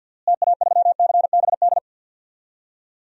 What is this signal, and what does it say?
Morse code, heard as a keyed tone, about 51 wpm, 700 Hz. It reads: TA3CBD
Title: DXCC-100 Call Signs - 6 Characters - Repeated 3x - 50wpm